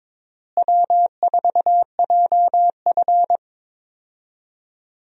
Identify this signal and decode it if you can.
Morse code, heard as a keyed tone, about 22 wpm, 700 Hz. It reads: W4JF
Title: USA Call Signs - 4 Characters - Repeated 3x - 22wpm